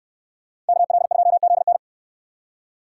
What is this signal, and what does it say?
Morse code, heard as a keyed tone, about 69 wpm, 700 Hz. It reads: KL2ZN